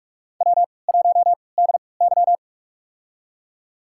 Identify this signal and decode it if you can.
Morse code, heard as a keyed tone, about 46 wpm, spaced as for 26 wpm, 700 Hz. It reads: W1DY
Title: USA Call Signs - 4 Characters - 3x Character Spacing - 45wpm